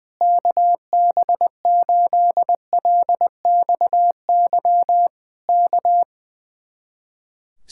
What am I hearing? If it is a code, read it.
Morse code, heard as a keyed tone, about 20 wpm, 700 Hz. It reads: KB8LXY K
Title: Speed-Racing Common QSO Phrases - 20wpm